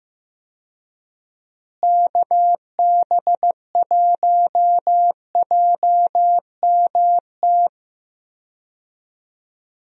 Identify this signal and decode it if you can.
Morse code, heard as a keyed tone, about 15 wpm, 700 Hz. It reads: KB1JMT
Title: USA Call Signs - 6 Characters - Repeated 3x - 15wpm